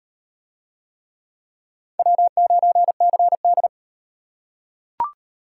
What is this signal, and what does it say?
Morse code, heard as a keyed tone, about 34 wpm, 700 Hz. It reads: W9CD